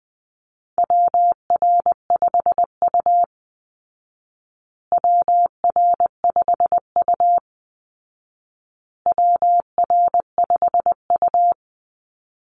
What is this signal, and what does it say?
Morse code, heard as a keyed tone, about 20 wpm, 700 Hz. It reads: WR5U WR5U WR5U
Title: Call Signs Repeated 2x - 20wpm